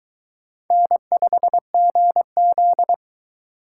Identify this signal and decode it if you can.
Morse code, heard as a keyed tone, about 22 wpm, 700 Hz. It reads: N5GZ